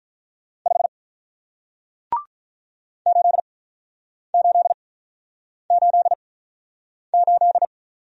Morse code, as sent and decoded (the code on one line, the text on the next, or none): ..... / ---.. ---.. ---.. ---..
5 8888